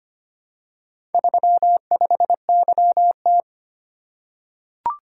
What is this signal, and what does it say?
Morse code, heard as a keyed tone, about 25 wpm, 700 Hz. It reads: 35YT
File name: practice / 25wpm / Four Letter-Number Combinations - 25wpm